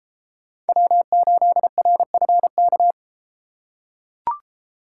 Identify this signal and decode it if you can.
Morse code, heard as a keyed tone, about 30 wpm, 700 Hz. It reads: W8RFK